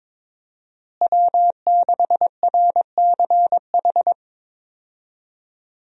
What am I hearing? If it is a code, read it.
Morse code, heard as a keyed tone, about 22 wpm, 700 Hz. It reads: W6RCH
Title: USA Call Signs - 5 Characters - Repeated 2x - 22wpm